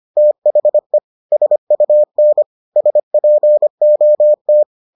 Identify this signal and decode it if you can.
Morse code, heard as a keyed tone, about 25 wpm, 600 Hz. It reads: THE SUN SPOT